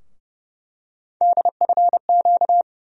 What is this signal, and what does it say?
Morse code, heard as a keyed tone, about 29 wpm, 700 Hz. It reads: DFQ
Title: USA Call Sign Postfixes - 3 Letters 30wpm